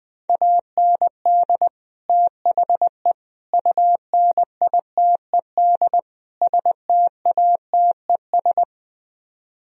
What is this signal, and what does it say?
Morse code, heard as a keyed tone, about 20 wpm, 700 Hz. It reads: AND THE UNITED STATES
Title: Speed-Racing - Sets of 4 Words 17wpm